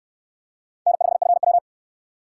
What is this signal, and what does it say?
Morse code, heard as a keyed tone, about 68 wpm, 700 Hz. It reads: N5FW